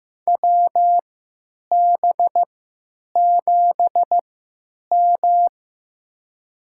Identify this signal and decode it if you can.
Morse code, heard as a keyed tone, about 16 wpm, spaced as for 9 wpm, 700 Hz. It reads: WB7M
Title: USA Call Signs - 3x Character Spacing - 15wpm